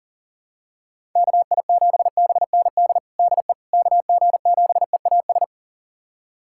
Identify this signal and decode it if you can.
Morse code, heard as a keyed tone, about 40 wpm, 700 Hz. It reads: KI7BND DE KG7EAS
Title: USA Callsign DE Callsign - 6 Characters - 40wpm